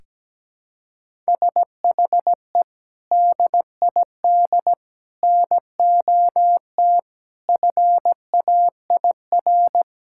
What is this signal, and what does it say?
Morse code, heard as a keyed tone, about 17 wpm, 700 Hz. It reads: SHE DID NOT FAIR
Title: Sentences from Top 1000 Words - Review Part 5 - 17wpm